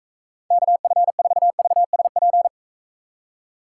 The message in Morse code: KF4VSP

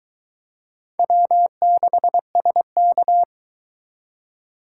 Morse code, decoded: W6SK

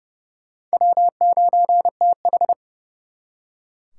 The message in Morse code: W9TH

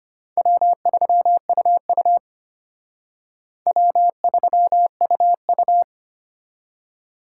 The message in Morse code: W3UU W3UU

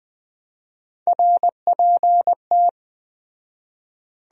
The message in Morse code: RPT

